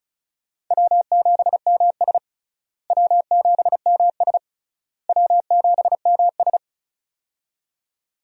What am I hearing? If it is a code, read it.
Morse code, heard as a keyed tone, about 36 wpm, 700 Hz. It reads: W7MS W7MS W7MS
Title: USA Call Signs - 4 Characters - Repeated 3x - 35wpm